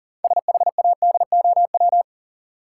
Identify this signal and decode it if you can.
Morse code, heard as a keyed tone, about 40 wpm, 700 Hz. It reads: SHADOW
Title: Sets of 5 Words Encore - New Word Drill - 40wpm